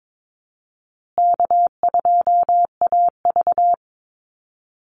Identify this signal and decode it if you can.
Morse code, heard as a keyed tone, about 22 wpm, 700 Hz. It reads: K2AV
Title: Easy Call Signs Repeated 2x - 22wpm